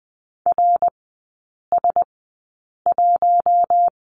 Morse code, heard as a keyed tone, about 20 wpm, spaced as for 8 wpm, 700 Hz. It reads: RS1
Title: 053 Lesson 10e - Letter r - Three Characters 20wpm